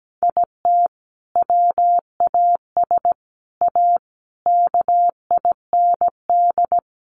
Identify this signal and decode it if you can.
Morse code, heard as a keyed tone, about 17 wpm, 700 Hz. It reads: IT WAS A KIND
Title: Sentences from Top 500 Words 17wpm